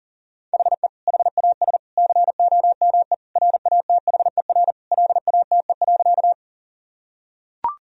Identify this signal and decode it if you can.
Morse code, heard as a keyed tone, about 37 wpm, 700 Hz. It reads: HE HAS COME RATHER LATE.
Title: Sentences from Top 1000 Words - Review Part 4 - 40wpm